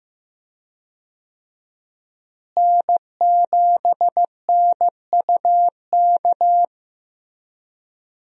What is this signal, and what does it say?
Morse code, heard as a keyed tone, about 15 wpm, 700 Hz. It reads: N7NUK
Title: USA Call Signs - 5 Characters - Repeated 3x - 15wpm